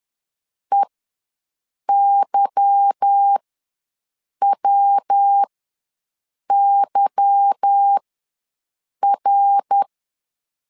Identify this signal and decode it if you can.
Morse code, heard as a keyed tone, about 10 wpm, spaced as for 6 wpm, 800 Hz. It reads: EYWYR